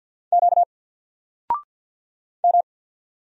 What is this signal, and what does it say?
Morse code, heard as a keyed tone, about 48 wpm, 700 Hz. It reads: Q M
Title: Single Letter-Number Alt 50wpm